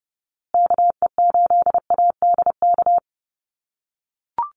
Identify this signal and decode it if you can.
Morse code, heard as a keyed tone, about 30 wpm, 700 Hz. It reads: KE8ADK
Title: USA Call Signs Repeated 2x 30wpm